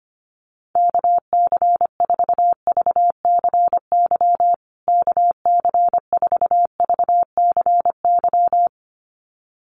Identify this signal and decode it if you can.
Morse code, heard as a keyed tone, about 25 wpm, 700 Hz. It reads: KC4VCY KC4VCY
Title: USA Call Signs Repeated 2x 25wpm